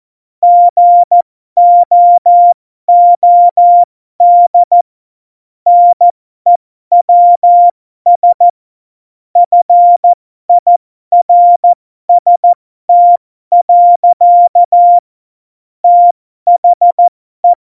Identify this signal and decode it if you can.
Morse code, heard as a keyed tone, about 14 wpm, spaced as for 12 wpm, 700 Hz. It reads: GOOD NEWS FIRST. THE